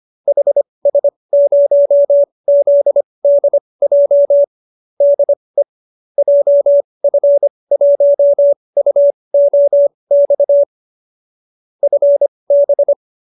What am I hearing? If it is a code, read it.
Morse code, heard as a keyed tone, about 25 wpm, spaced as for 20 wpm, 570 Hz. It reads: HS0ZDJ DE JF1UOX FB
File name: qso216b 20